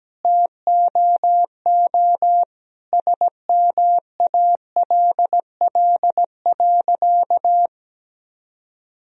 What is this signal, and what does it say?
Morse code, heard as a keyed tone, about 17 wpm, 700 Hz. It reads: TOO SMALL.
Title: Sentences from Top 1000 Words - Review Part 1 - 17wpm